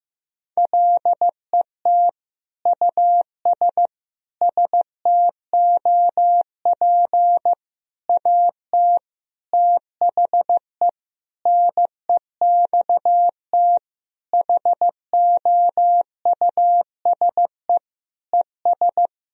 Morse code, heard as a keyed tone, about 16 wpm, 700 Hz. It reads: LET US STOP AT THE NEXT HOUSE ES
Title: Sentences from Top 400 Words and Abbreviations 15wpm